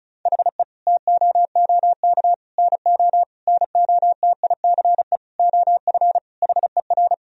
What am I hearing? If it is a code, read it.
Morse code, heard as a keyed tone, about 34 wpm, 700 Hz. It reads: HE TOOK NO NOTICE OF HER
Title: Sentences from Top 1000 Words - Review Part 2 - 35wpm